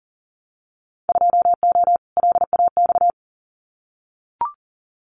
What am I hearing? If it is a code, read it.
Morse code, heard as a keyed tone, about 38 wpm, 700 Hz. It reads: 2O LAX
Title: Winter Field Day Exchanges 40wpm